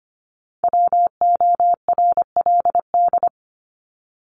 Morse code, heard as a keyed tone, about 25 wpm, 700 Hz. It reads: WORLD
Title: Top 200 Words 25wpm